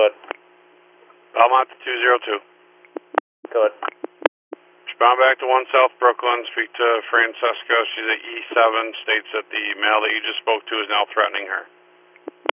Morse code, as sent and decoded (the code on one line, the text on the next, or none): none